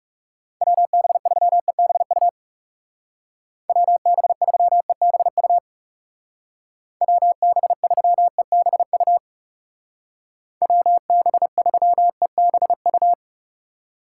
Morse code, heard as a keyed tone, about 36 wpm, 700 Hz. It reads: WB3EBU WB3EBU WB3EBU WB3EBU
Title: Speed Racing - USA 2x3 Call Signs 30wpm